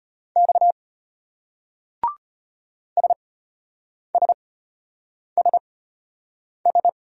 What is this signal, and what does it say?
Morse code, heard as a keyed tone, about 33 wpm, spaced as for 8 wpm, 700 Hz. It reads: X SSSS